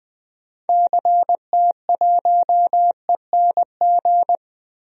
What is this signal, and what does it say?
Morse code, heard as a keyed tone, about 21 wpm, 700 Hz. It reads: CT1ENG
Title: DXCC-100 Call Signs - 6 Characters - Repeated 2x - 20wpm